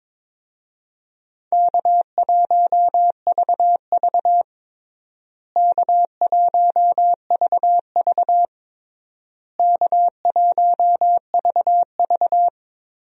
Morse code, heard as a keyed tone, about 22 wpm, 700 Hz. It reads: K1VV K1VV K1VV
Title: USA Call Signs - 4 Characters - Repeated 3x - 22wpm